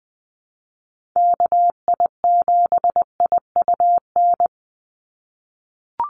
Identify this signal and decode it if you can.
Morse code, heard as a keyed tone, about 20 wpm, 700 Hz. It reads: KI7IUN
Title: USA Call Signs 20wpm